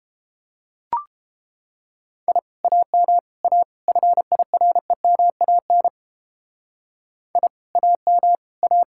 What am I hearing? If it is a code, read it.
Morse code, heard as a keyed tone, about 29 wpm, 700 Hz. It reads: I AM A FIREMAN I AM A